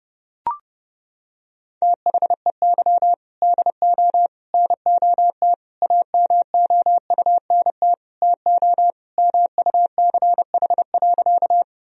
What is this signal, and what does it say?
Morse code, heard as a keyed tone, about 30 wpm, 700 Hz. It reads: THEY DO NOT AMOUNT TO MUCH.
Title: Sentences from Top 1000 Words - Review Part 4 - 30wpm